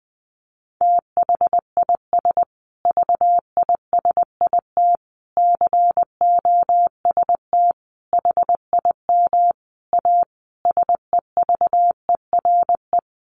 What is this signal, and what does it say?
Morse code, heard as a keyed tone, about 20 wpm, 700 Hz. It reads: THIS VISIT COST HIM A SEVERE
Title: Sentences from Top 1400 Words 20wpm